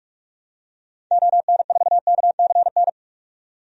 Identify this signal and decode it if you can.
Morse code, heard as a keyed tone, about 41 wpm, 700 Hz. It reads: ON4KCN
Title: DXCC-100 Call Signs - 6 Characters - 45wpm